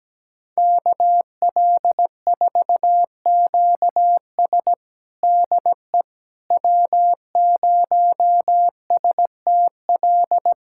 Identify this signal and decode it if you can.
Morse code, heard as a keyed tone, about 17 wpm, 700 Hz. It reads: KL4QS DE W0STL
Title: USA Callsign DE Callsign - 5 Characters - 17wpm